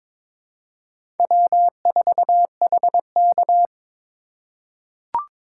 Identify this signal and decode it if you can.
Morse code, heard as a keyed tone, about 21 wpm, 700 Hz. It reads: W4HK
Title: USA Call Signs - 4 Characters - 4x Character Spacing - 22wpm